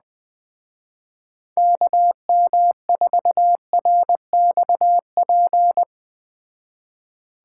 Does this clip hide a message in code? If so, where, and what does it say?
Yes: Morse code from 1.6 to 5.8 s, heard as a keyed tone, about 20 wpm, 700 Hz. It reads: KM4RXP